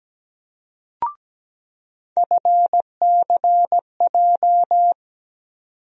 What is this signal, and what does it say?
Morse code, heard as a keyed tone, about 17 wpm, 700 Hz. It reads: FCJ
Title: USA Call Sign Postfixes - 3 Letters 17wpm